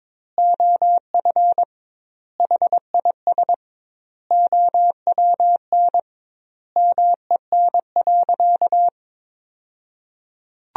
Morse code, heard as a keyed tone, about 22 wpm, 700 Hz. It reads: OF HIS OWN MEN.
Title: 2x Word-Spacing - Sentences from Top 300 Words - 22wpm